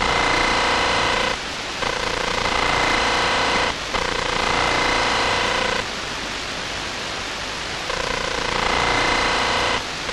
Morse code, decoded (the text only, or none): none